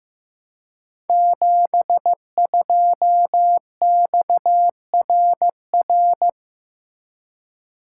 Morse code, heard as a keyed tone, about 15 wpm, 700 Hz. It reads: 72XRR
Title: Code Groups - 5 Letter:Numbers - 15wpm